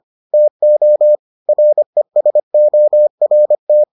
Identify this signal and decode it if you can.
Morse code, heard as a keyed tone, about 25 wpm, 600 Hz. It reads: TO RESORT